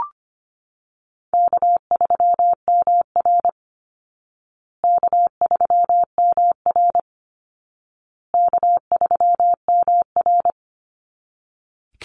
Morse code, heard as a keyed tone, about 25 wpm, 700 Hz. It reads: K3MR K3MR K3MR
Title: Easy Call Signs Repeated 3x - 25wpm